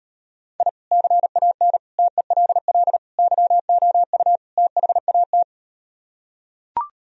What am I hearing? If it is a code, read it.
Morse code, heard as a keyed tone, about 38 wpm, 700 Hz. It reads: I CAN TELL YOU THAT